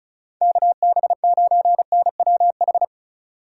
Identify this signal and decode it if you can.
Morse code, heard as a keyed tone, about 45 wpm, 700 Hz. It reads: KB9NWH